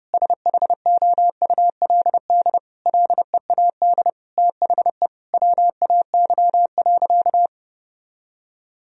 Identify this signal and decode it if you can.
Morse code, heard as a keyed tone, about 30 wpm, 700 Hz. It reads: SHOULD LEAD THE WAY.
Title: Sentences from Top 1000 Words - Review Part 2 - 30wpm